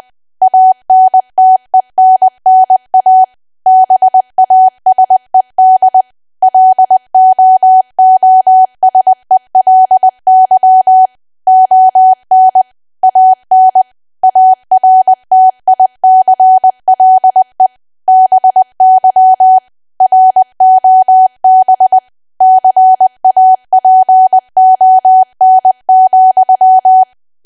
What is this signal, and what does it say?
Morse code, heard as a keyed tone, about 20 wpm, 750 Hz. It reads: ANTENNA BASED LOOSELY ON AN ARTICLE BY ROB CAPON,
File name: cwa / W1AW-Level3 / 20WPM